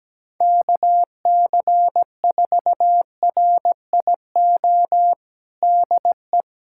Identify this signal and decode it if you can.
Morse code, heard as a keyed tone, about 17 wpm, 700 Hz. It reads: KC4RIO DE
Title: USA Callsign DE Callsign 17wpm